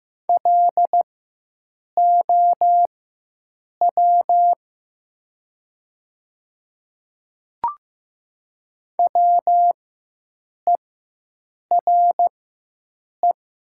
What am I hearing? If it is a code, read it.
Morse code, heard as a keyed tone, about 15 wpm, spaced as for 7 wpm, 700 Hz. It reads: LOW WERE
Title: 4x Character Spacing - Top 200 Words - 15wpm